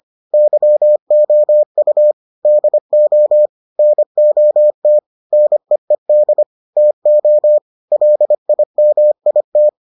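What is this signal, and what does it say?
Morse code, heard as a keyed tone, about 25 wpm, 600 Hz. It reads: YOU DO NOT NEED TO LIMIT
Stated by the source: Morse edition of the Foundations of Amateur Radio episodes.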